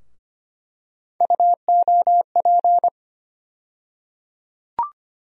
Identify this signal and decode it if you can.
Morse code, heard as a keyed tone, about 25 wpm, 700 Hz. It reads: UOP
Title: USA Call Sign Postfixes - 3 Letters 25wpm